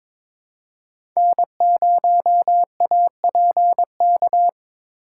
Morse code, heard as a keyed tone, about 22 wpm, 700 Hz. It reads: N0APK
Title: USA Call Signs - 5 Characters - 4x Character Spacing - 22wpm